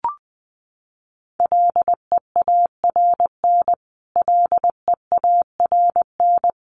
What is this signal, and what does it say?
Morse code, heard as a keyed tone, about 20 wpm, 700 Hz. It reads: LEARN LEARN
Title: Top 500 Words Repeated 20wpm